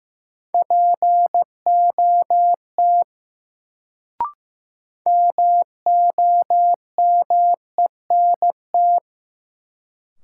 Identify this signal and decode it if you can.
Morse code, heard as a keyed tone, about 15 wpm, 700 Hz. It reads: POT MOMENT
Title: Sentences from Top 1000 Words - New Word Drill - 15wpm